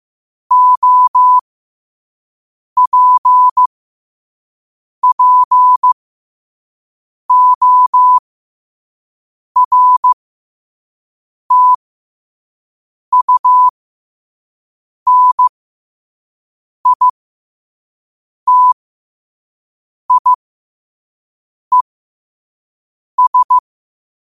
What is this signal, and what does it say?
Morse code, heard as a keyed tone, about 15 wpm, spaced as for 5 wpm, 1000 Hz. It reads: OPPORTUNITIES
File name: qotdfiles / 5wpm